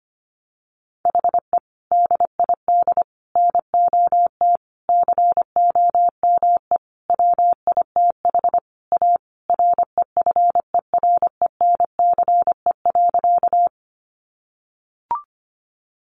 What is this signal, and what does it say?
Morse code, heard as a keyed tone, about 25 wpm, 700 Hz. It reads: HE DID NOT COME WITH A REFERENCE.
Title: Sentences from Top 1200 Words 25wpm